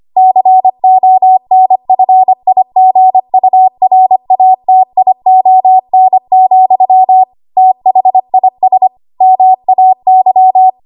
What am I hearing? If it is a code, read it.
Morse code, heard as a keyed tone, about 25 wpm, 750 Hz. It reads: CONFIGURATION, THIS MAY
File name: Archive / 25 WPM / 25WPM